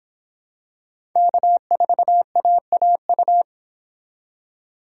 Morse code, heard as a keyed tone, about 25 wpm, 700 Hz. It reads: K4AAU